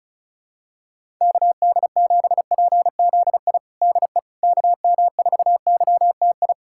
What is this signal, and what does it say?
Morse code, heard as a keyed tone, about 34 wpm, 700 Hz. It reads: KD7PZI DE KM4YTI
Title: USA Callsign DE Callsign - 6 Characters - 35wpm